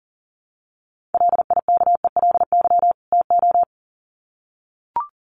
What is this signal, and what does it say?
Morse code, heard as a keyed tone, about 40 wpm, 700 Hz. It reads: LIKELY TO
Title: Sets of 2 Words 40wpm